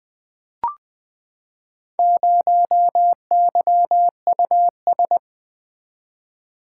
Morse code, heard as a keyed tone, about 20 wpm, 700 Hz. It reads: 0YUS
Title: Four Letter-Number Combinations - 20wpm